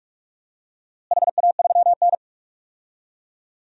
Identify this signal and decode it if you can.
Morse code, heard as a keyed tone, about 46 wpm, 700 Hz. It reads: HA3N
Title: DXCC-100 Call Signs - 4 Characters - Repeated 3x - 45wpm